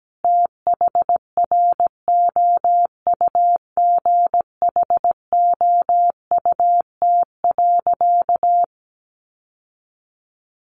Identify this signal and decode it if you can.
Morse code, heard as a keyed tone, about 17 wpm, 700 Hz. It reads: THROUGHOUT.
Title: Sentences from Top 800 Words 17wpm